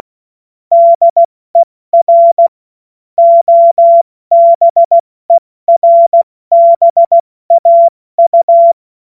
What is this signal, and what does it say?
Morse code, heard as a keyed tone, about 16 wpm, 680 Hz. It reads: DER OBERBAU
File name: Morsekurs / Witze / Witz 21. Juni 2022 70 F80